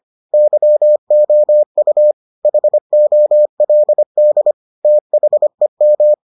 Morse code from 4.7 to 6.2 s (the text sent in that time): THEM